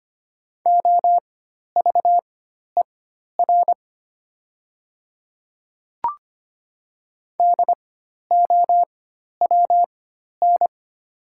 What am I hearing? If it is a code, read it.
Morse code, heard as a keyed tone, about 25 wpm, spaced as for 12 wpm, 700 Hz. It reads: OVER DOWN
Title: 4x Character Spacing - Top 300 Words - 25wpm